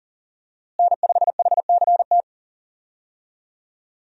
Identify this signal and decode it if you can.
Morse code, heard as a keyed tone, about 40 wpm, 700 Hz. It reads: N5HCT
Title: USA Call Signs - 5 Characters - Repeated 3x - 40wpm